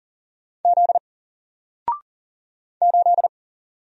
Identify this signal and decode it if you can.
Morse code, heard as a keyed tone, about 40 wpm, 700 Hz. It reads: Z 8
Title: Single Letter-Number Alt 40wpm